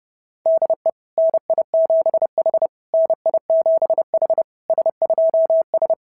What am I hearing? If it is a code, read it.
Morse code, heard as a keyed tone, about 26 wpm, 650 Hz. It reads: DE NI7H NI7H S2S
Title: SOTA QSO Practice 30wpm